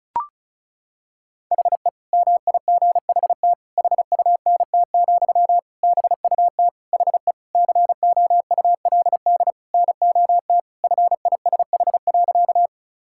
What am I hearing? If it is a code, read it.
Morse code, heard as a keyed tone, about 34 wpm, 700 Hz. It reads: HE MIGHT HUNT, BUT HE COULD NOT FISH.
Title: Sentences from Top 1600 Words 35wpm